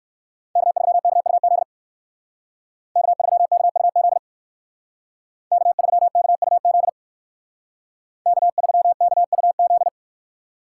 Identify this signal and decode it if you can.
Morse code, heard as a keyed tone, about 54 wpm, 700 Hz. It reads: K3KUZ K3KUZ K3KUZ K3KUZ
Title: Speed Racing - USA 1x3 Call Signs 45wpm